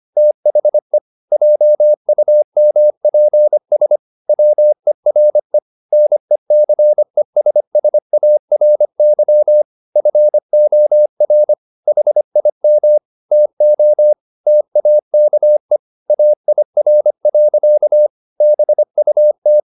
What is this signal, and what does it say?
Morse code, heard as a keyed tone, about 25 wpm, 600 Hz. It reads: THE JUMPS WERE NECESSARY FOR HIM TO TAKE AIR. BUT